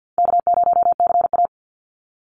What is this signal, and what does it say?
Morse code, heard as a keyed tone, about 50 wpm, 700 Hz. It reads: K0CA
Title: Easy Call Signs - 50wpm